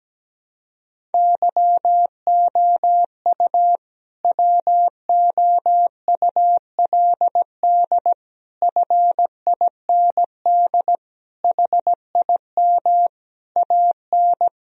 Morse code, heard as a keyed tone, about 17 wpm, 700 Hz. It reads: YOU WOULD FIND HIM AN
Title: Sentences from Top 1000 Words - Review Part 1 - 17wpm